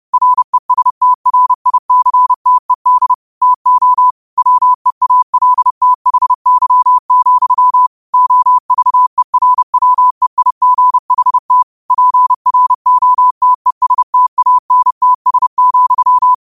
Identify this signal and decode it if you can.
Morse code, heard as a keyed tone, about 30 wpm, 1000 Hz. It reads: RESTRICTED TO WEALTHY, OVERWEIGHT PROTESTANTS,
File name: qotdfiles / 30wpm